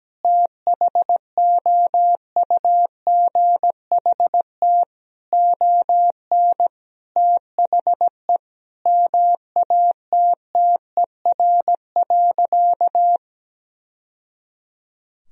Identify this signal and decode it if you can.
Morse code, heard as a keyed tone, about 17 wpm, 700 Hz. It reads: THOUGHT ON THE MATTER.
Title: Sentences from Top 1000 Words - Review Part 1 - 17wpm